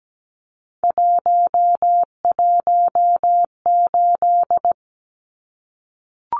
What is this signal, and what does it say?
Morse code, heard as a keyed tone, about 17 wpm, 700 Hz. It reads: 118
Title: Numbers - 3-Digit 17wpm